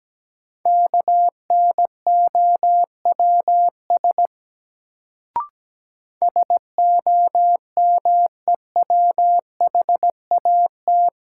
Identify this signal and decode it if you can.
Morse code, heard as a keyed tone, about 17 wpm, 700 Hz. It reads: KNOWS SOMEWHAT
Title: Sentences from Top 1000 Words - Review Part 1 - New Word Drill - 17wpm